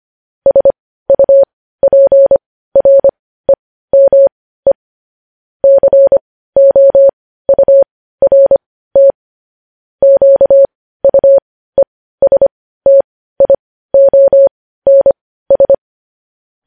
Morse code, heard as a keyed tone, about 25 wpm, spaced as for 15 wpm, 550 Hz. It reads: SUPREME COURT QUESTIONS